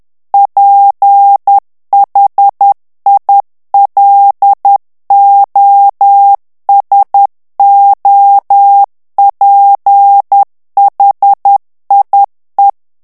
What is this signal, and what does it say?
Morse code, heard as a keyed tone, about 10 wpm, 800 Hz. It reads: PHILOSOPHIE
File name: morse mp3 / FRANCAIS TEXTE CLAIR 2 VITESSE 10 MOTS MINUTES